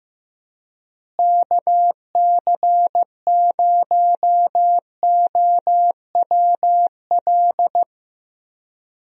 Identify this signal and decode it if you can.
Morse code, heard as a keyed tone, about 15 wpm, 700 Hz. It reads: KC0OWL
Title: USA Call Signs - 6 Characters - Repeated 3x - 15wpm